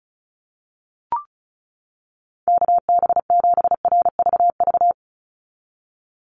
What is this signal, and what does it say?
Morse code, heard as a keyed tone, about 34 wpm, 700 Hz. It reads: KB7RVV
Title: USA Call Signs 35wpm